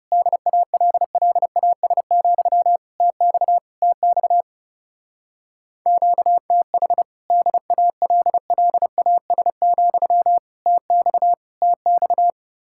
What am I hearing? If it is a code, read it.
Morse code, heard as a keyed tone, about 36 wpm, 700 Hz. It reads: DALLAS, TX TX QTH DALLAS, TX TX
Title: Speed-Racing Common QSO Phrases - 30wpm